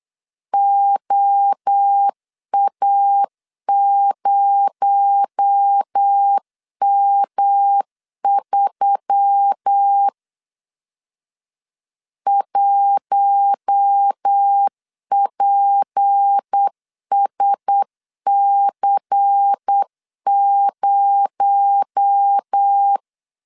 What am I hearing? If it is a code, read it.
Morse code, heard as a keyed tone, about 8 wpm, 800 Hz. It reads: OA0M3 1PSC0
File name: LEZIONI / 6 PKX0 / 6 PKX0 8wpm 3341